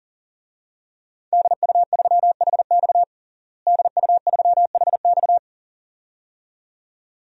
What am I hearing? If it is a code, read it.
Morse code, heard as a keyed tone, about 37 wpm, 700 Hz. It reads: DU3HX DU3HX
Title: DXCC-100 Call Signs - 5 Characters - Repeated 2x - 40wpm